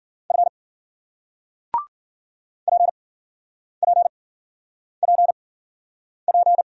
Morse code, heard as a keyed tone, about 53 wpm, spaced as for 9 wpm, 700 Hz. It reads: F PPPP